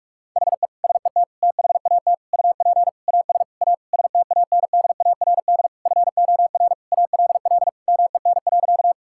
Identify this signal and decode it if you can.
Morse code, heard as a keyed tone, about 44 wpm, 700 Hz. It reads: HE SET THAT UP AS A STANDARD FOR ALL MEN.
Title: Sentences from Top 1000 Words - Review Part 6 - 45wpm